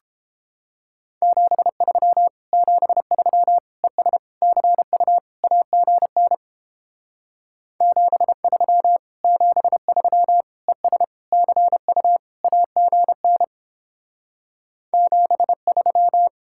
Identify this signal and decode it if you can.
Morse code, heard as a keyed tone, about 31 wpm, 700 Hz. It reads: 73 73 ES CU AGN 73 73 ES CU AGN 73